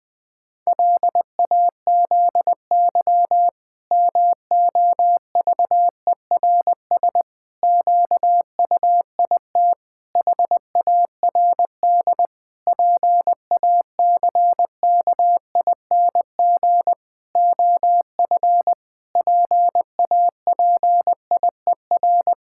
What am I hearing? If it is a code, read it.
Morse code, heard as a keyed tone, about 21 wpm, 700 Hz. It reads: LAZY MOVERS QUIT HARD PACKING OF PAPIER
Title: Pangram Challenge v2 - 20wpm